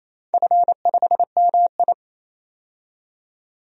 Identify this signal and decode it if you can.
Morse code, heard as a keyed tone, about 28 wpm, 700 Hz. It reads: F5MI
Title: DXCC-100 Call Signs - 4 Characters - 28wpm